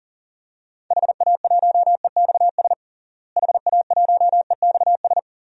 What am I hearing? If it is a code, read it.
Morse code, heard as a keyed tone, about 40 wpm, 700 Hz. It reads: HA1EXS HA1EXS
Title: DXCC-100 Call Signs - 6 Characters - Repeated 3x - 40wpm